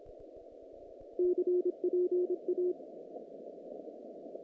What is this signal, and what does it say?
Morse code, heard as a keyed tone, about 26 wpm, 360 Hz. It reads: CPA
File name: contests / naqp-cw-k5tr-op-k5pi- -2200